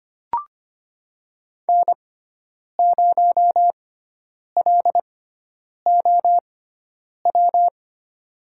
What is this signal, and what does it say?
Morse code, heard as a keyed tone, about 25 wpm, spaced as for 9 wpm, 700 Hz. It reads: N0LOW
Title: USA Call Signs - 6x Character Spacing - 25wpm